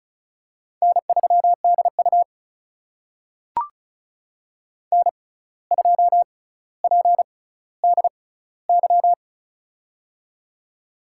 Morse code, heard as a keyed tone, about 34 wpm, spaced as for 17 wpm, 700 Hz. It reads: N3DU N2PDY
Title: USA Call Signs - 6x Character Spacing - 35wpm